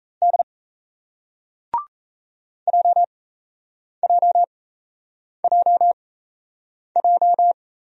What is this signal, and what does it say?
Morse code, heard as a keyed tone, about 38 wpm, spaced as for 8 wpm, 700 Hz. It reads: D JJJJ